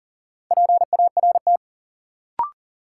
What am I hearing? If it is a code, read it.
Morse code, heard as a keyed tone, about 40 wpm, 700 Hz. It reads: PART
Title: Sentences from Top 200 Words - New Word Drill - 40wpm